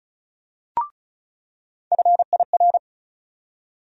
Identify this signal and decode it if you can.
Morse code, heard as a keyed tone, about 34 wpm, 700 Hz. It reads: FIR
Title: USA Call Sign Postfixes - 3 Letters 35wpm